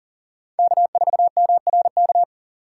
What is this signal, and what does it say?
Morse code, heard as a keyed tone, about 42 wpm, 700 Hz. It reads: K4MRK
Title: USA Call Signs - 5 Characters - 6x Character Spacing - 40wpm